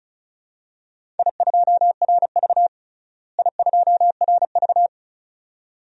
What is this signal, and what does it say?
Morse code, heard as a keyed tone, about 36 wpm, 700 Hz. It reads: I2RV I2RV
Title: DXCC-100 Call Signs - 4 Characters - Repeated 2x - 35wpm